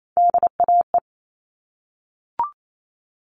Morse code, heard as a keyed tone, about 28 wpm, 700 Hz. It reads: DAE
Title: All Cut Numbers - 3-Digit 28wpm